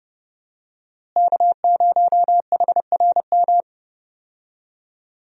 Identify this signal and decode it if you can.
Morse code, heard as a keyed tone, about 30 wpm, 700 Hz. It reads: K0HRM
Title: USA Call Signs - 5 Characters - Repeated 2x - 30wpm